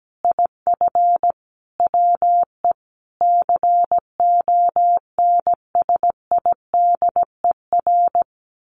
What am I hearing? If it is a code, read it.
Morse code, heard as a keyed tone, about 17 wpm, 700 Hz. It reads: IF WE CONSIDER
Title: Sentences from Top 1100 Words 17wpm